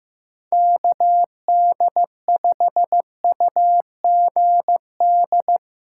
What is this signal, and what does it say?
Morse code, heard as a keyed tone, about 15 wpm, 700 Hz. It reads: KD5UGD
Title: USA Call Signs - 6 Characters - 6x Character Spacing - 15wpm